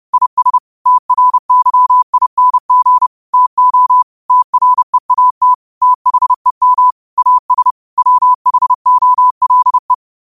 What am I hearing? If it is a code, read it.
Morse code, heard as a keyed tone, about 30 wpm, 1000 Hz. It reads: IS TRYING TO TREAT THEM AS WHOLE